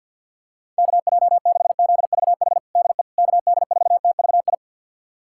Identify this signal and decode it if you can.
Morse code, heard as a keyed tone, about 51 wpm, 700 Hz. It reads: KJ6ZVH DE KD4TVI
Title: USA Callsign DE Callsign 50wpm